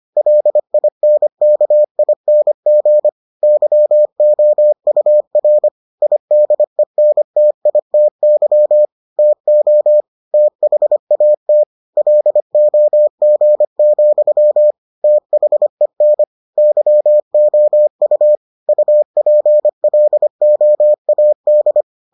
Morse code, heard as a keyed tone, about 25 wpm, 600 Hz. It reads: LINKING YOUR IDENTITY TO THAT LOG, THEN YOU UPLOAD